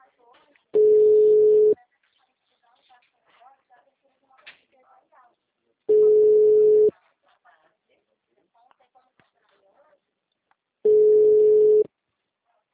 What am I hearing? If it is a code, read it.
Ringback tone (Europe, 425 Hz), with speech.